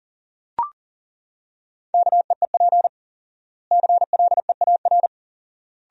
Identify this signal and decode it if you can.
Morse code, heard as a keyed tone, about 40 wpm, 700 Hz. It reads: KEEP CLEAR
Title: 4x Word-Spacing - Sentences from Top 500 Words - 40wpm